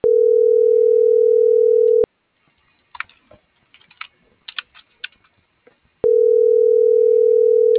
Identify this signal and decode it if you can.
Ringback tone (North America), 440 + 480 Hz.
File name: CENTRAL MABE / out edwin.sanchez